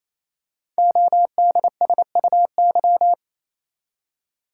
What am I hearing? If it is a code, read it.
Morse code, heard as a keyed tone, about 29 wpm, 700 Hz. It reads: ODSUY
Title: Code Groups - 5 Letters - 28wpm